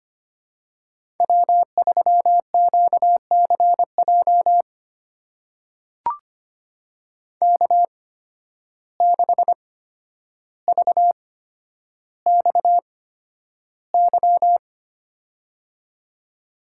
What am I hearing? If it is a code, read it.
Morse code, heard as a keyed tone, about 25 wpm, spaced as for 11 wpm, 700 Hz. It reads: W3QCJ K6VXY
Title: USA Call Signs - 8x Character Spacing - 25wpm